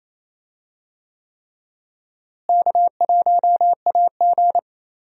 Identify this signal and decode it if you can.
Morse code, heard as a keyed tone, about 28 wpm, 700 Hz. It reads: K1AG
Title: USA Call Signs - 4 Characters - Repeated 3x - 28wpm